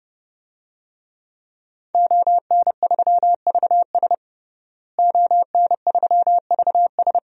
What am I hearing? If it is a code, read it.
Morse code, heard as a keyed tone, about 30 wpm, 700 Hz. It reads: ON3VS ON3VS
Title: DXCC-100 Call Signs - 5 Characters - Repeated 3x - 30wpm